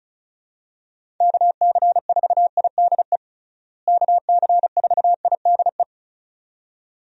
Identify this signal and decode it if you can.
Morse code, heard as a keyed tone, about 36 wpm, 700 Hz. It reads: KC4IDE KC4IDE
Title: USA Call Signs - 6 Characters - Repeated 2x - 35wpm